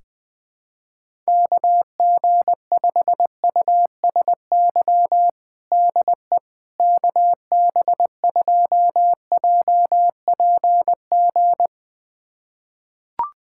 Morse code, heard as a keyed tone, about 21 wpm, 700 Hz. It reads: KG5USY DE KB2JPG
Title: USA Callsign DE Callsign - 6 Characters - 20wpm